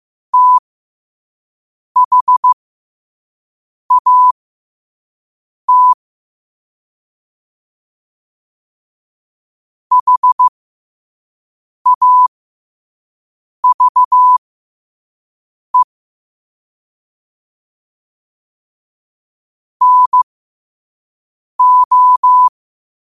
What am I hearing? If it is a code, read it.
Morse code, heard as a keyed tone, about 15 wpm, spaced as for 5 wpm, 1000 Hz. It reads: THAT HAVE NO
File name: qotdfiles / 5wpm